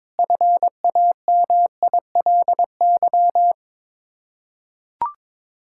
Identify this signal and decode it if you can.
Morse code, heard as a keyed tone, about 22 wpm, 700 Hz. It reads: FAMILY
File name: practice / 22wpm / Sets of 7 Words - 22wpm